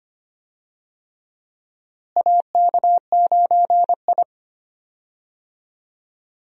Morse code, heard as a keyed tone, about 25 wpm, 700 Hz. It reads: AK9I
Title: USA Call Signs - 4 Characters - Repeated 3x - 25wpm